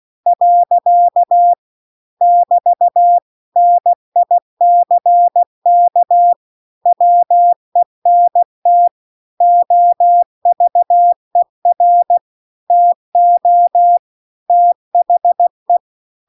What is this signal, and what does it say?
Morse code, heard as a keyed tone, about 16 wpm, 700 Hz. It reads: . <BT> NICK WENT OVER TO THE